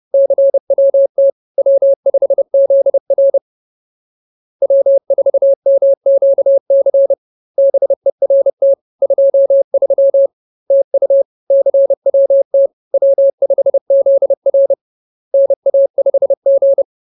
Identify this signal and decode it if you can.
Morse code, heard as a keyed tone, about 30 wpm, 550 Hz. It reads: CWT W5ZR W4MQC BERT 23 TU CWT W5ZR NA5G